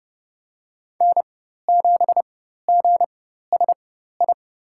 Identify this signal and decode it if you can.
Morse code, heard as a keyed tone, about 32 wpm, spaced as for 14 wpm, 700 Hz. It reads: N7GSI
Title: USA Call Signs - 5 Characters - 4x Character Spacing - 30wpm